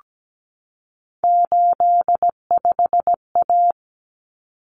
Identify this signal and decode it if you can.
Morse code, heard as a keyed tone, about 17 wpm, 700 Hz. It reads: 85A
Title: Common Cut Numbers - 3-Digit 17wpm